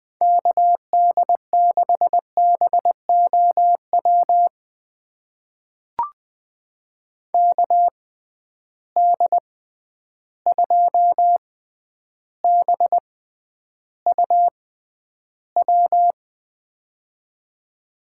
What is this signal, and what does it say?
Morse code, heard as a keyed tone, about 20 wpm, spaced as for 10 wpm, 700 Hz. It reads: KD6BOW KD2BUW
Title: USA Call Signs - 6 Characters - 6x Character Spacing - 20wpm